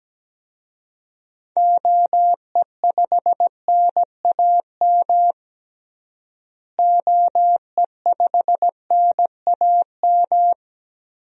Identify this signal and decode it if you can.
Morse code, heard as a keyed tone, about 17 wpm, 700 Hz. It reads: OE5NAM OE5NAM
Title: DXCC-100 Call Signs - 6 Characters - Repeated 2x - 17wpm